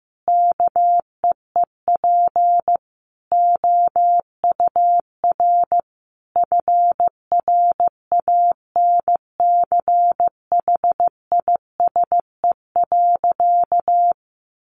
Morse code, heard as a keyed tone, about 15 wpm, 700 Hz. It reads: KEEP OUR FRANCHISE.
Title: A Sound Of Thunder 15wpm